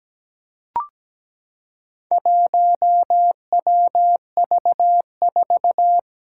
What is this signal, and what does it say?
Morse code, heard as a keyed tone, about 17 wpm, 700 Hz. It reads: 1WV4
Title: Four Challenging Letter-Number Combinations 17wpm